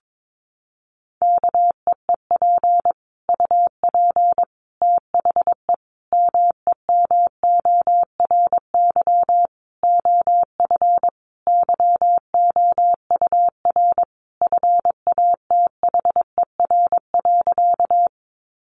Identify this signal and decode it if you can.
Morse code, heard as a keyed tone, about 22 wpm, 700 Hz. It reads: KEEP UP THE MEMORY OF YOUR FATHER.
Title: Sentences from Top 1300 Words 22wpm